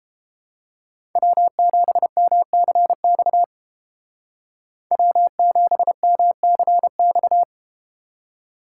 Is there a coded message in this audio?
Yes: Morse code from 1.2 to 7.4 s, heard as a keyed tone, about 31 wpm, 700 Hz. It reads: W7MCX W7MCX